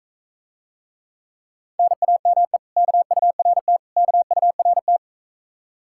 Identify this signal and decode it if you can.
Morse code, heard as a keyed tone, about 43 wpm, 700 Hz. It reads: NAME KURT KURT